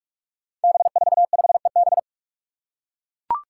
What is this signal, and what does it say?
Morse code, heard as a keyed tone, about 41 wpm, 700 Hz. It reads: B45EB
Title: Five Challenging Letter-Number Combinations 45wpm